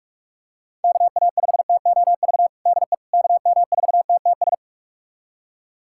Morse code, heard as a keyed tone, about 45 wpm, 690 Hz. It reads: KA5TOV DE KM4TTS
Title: USA Callsign DE Callsign - 6 Characters - 45wpm